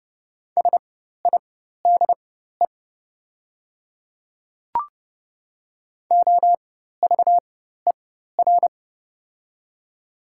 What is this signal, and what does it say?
Morse code, heard as a keyed tone, about 30 wpm, spaced as for 14 wpm, 700 Hz. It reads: SIDE OVER
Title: 4x Character Spacing - Top 200 Words - 30wpm